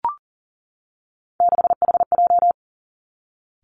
Morse code, heard as a keyed tone, about 40 wpm, 700 Hz. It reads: 6HJ
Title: Three Challenging Letter-Number Combinations 40wpm